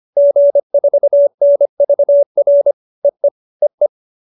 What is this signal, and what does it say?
Morse code, heard as a keyed tone, about 25 wpm, 570 Hz. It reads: G4NVR EE EE
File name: QSO315 25